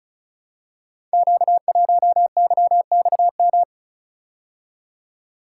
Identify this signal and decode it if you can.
Morse code, heard as a keyed tone, about 36 wpm, 700 Hz. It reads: Q1YXM
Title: Code Groups - 5 Letter:Numbers - 35wpm